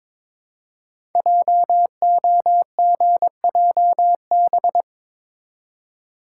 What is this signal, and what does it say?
Morse code, heard as a keyed tone, about 23 wpm, 700 Hz. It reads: JOGJB
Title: Code Groups - 5 Letters - 22wpm